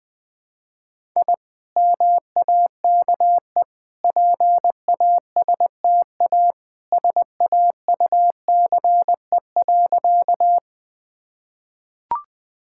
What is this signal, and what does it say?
Morse code, heard as a keyed tone, about 20 wpm, 700 Hz. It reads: I MAKE PASTA SAUCE.
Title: Easy Sentences with Hints - 20wpm